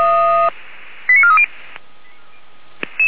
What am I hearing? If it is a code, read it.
Five-tone selective call (ZVEI), heard as quick runs of five tones, tones of about 70 ms. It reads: 89430